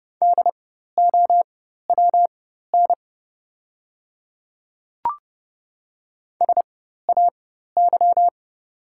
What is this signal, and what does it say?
Morse code, heard as a keyed tone, about 30 wpm, spaced as for 14 wpm, 700 Hz. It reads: DOWN SAY
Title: 4x Character Spacing - Top 300 Words - 30wpm